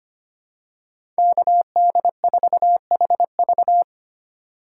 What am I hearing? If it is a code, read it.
Morse code, heard as a keyed tone, about 25 wpm, 700 Hz. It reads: KD4HV
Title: USA Call Signs - 5 Characters - 3x Character Spacing - 25wpm